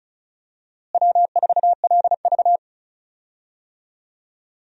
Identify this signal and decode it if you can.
Morse code, heard as a keyed tone, about 34 wpm, 700 Hz. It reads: W4LV
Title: USA Call Signs - 4 Characters - Repeated 2x - 35wpm